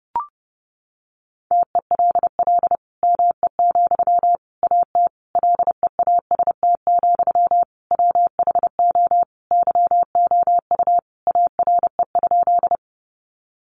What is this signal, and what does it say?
Morse code, heard as a keyed tone, about 28 wpm, 700 Hz. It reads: TELL ME, AT LEAST, WHO YOU ARE?
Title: Sentences from Top 700 Words 30wpm